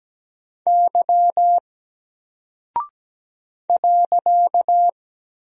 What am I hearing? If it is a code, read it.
Morse code, heard as a keyed tone, about 17 wpm, 700 Hz. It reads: Y .
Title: Single Character (Letter:Number:Punctuation:Prosign) Alt 17wpm